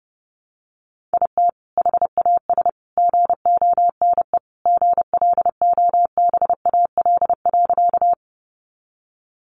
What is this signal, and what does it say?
Morse code, heard as a keyed tone, about 30 wpm, 700 Hz. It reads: IT HAS GONE GLOBAL.
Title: Sentences from Top 1200 Words 30wpm